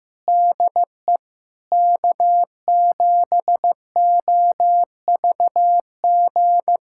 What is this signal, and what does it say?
Morse code, heard as a keyed tone, about 15 wpm, 700 Hz. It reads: DE K7OVG
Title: USA Callsign DE Callsign 15wpm